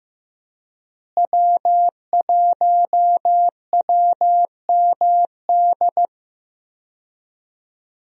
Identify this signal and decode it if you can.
Morse code, heard as a keyed tone, about 16 wpm, 700 Hz. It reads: W1WMD
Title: USA Call Signs - 5 Characters - Repeated 3x - 15wpm